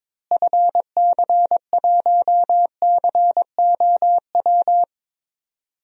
Morse code, heard as a keyed tone, about 22 wpm, 700 Hz. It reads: FC1COW
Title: DXCC-100 Call Signs - 6 Characters - Repeated 3x - 22wpm